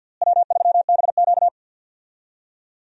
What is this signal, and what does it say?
Morse code, heard as a keyed tone, about 52 wpm, 700 Hz. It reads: W3BQ
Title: USA Call Signs - 4 Characters - Repeated 2x - 50wpm